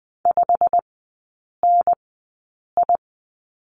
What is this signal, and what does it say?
Morse code, heard as a keyed tone, about 20 wpm, spaced as for 8 wpm, 690 Hz. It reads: 5NI
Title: 093 Lesson 16e - Letter c - Three Characters 20wpm